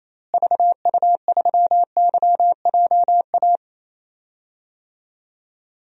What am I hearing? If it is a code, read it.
Morse code, heard as a keyed tone, about 28 wpm, 700 Hz. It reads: VU3YJA
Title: DXCC-100 Call Signs - 6 Characters - Repeated 2x - 28wpm